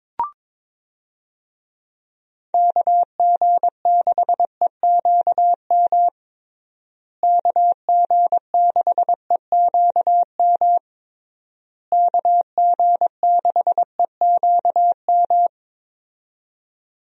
Morse code, heard as a keyed tone, about 22 wpm, 700 Hz. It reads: KG6EQM KG6EQM KG6EQM
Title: USA Call Signs - 6 Characters - Repeated 3x - 22wpm